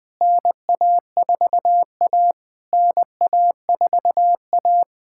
Morse code, heard as a keyed tone, about 20 wpm, 700 Hz. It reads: NA4A NA4A
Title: POTA QSO Practice - 20wpm